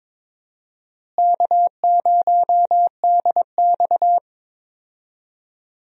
Morse code, heard as a keyed tone, about 21 wpm, 700 Hz. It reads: K0DX
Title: USA Call Signs - 4 Characters - Repeated 2x - 22wpm